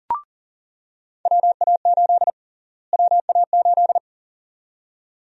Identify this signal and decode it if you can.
Morse code, heard as a keyed tone, about 40 wpm, 700 Hz. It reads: WA8 WA8
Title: USA Call Sign Prefixes - Repeated 2x - 40wpm